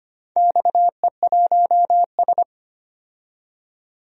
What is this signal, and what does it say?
Morse code, heard as a keyed tone, about 25 wpm, 700 Hz. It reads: XE1S
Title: DXCC-100 Call Signs - 4 Characters - 25wpm